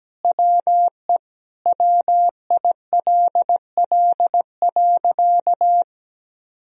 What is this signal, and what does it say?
Morse code, heard as a keyed tone, about 17 wpm, 700 Hz. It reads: WE WILL.
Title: Sentences from Top 1000 Words - Review Part 1 - 17wpm